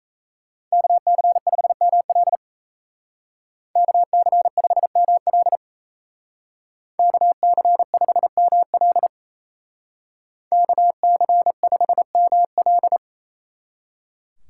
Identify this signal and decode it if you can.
Morse code, heard as a keyed tone, about 36 wpm, 700 Hz. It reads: KC5ML KC5ML KC5ML KC5ML